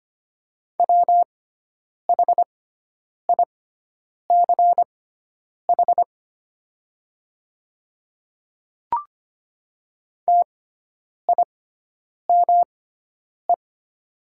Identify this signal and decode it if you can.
Morse code, heard as a keyed tone, about 25 wpm, spaced as for 9 wpm, 700 Hz. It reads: WHICH TIME
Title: 6x Character Spacing - Top 100 Words - 25wpm